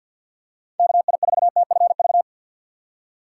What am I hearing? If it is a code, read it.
Morse code, heard as a keyed tone, about 47 wpm, 700 Hz. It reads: KI4TFV
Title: USA Call Signs - 6 Characters - 4x Character Spacing - 50wpm